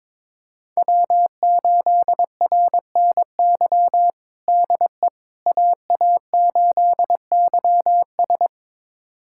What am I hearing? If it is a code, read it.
Morse code, heard as a keyed tone, about 22 wpm, 700 Hz. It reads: W8RNY DE AA8YS
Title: USA Callsign DE Callsign 22wpm